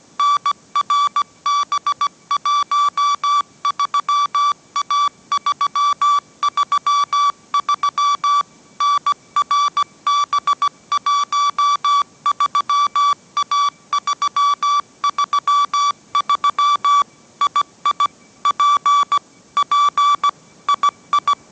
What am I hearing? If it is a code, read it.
Morse code, heard as a keyed tone, about 17 wpm, 1150 Hz. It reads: NRB13A333 NRB13A333 II P P II